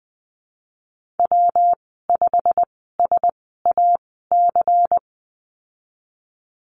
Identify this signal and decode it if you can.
Morse code, heard as a keyed tone, about 20 wpm, spaced as for 15 wpm, 700 Hz. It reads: W5SAC
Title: 154 Lesson 25g - Letter p - Call Signs 20wpm